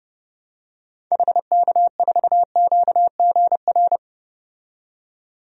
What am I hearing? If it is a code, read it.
Morse code, heard as a keyed tone, about 30 wpm, 690 Hz. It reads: HK4QGR
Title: DXCC-100 Call Signs - 6 Characters - 30wpm